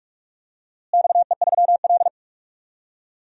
Morse code, heard as a keyed tone, about 46 wpm, 700 Hz. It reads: XE3L
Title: DXCC-100 Call Signs - 4 Characters - Repeated 3x - 45wpm